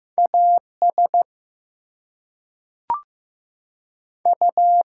Morse code, heard as a keyed tone, about 15 wpm, 700 Hz. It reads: AS U
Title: Two Letter Words - 15wpm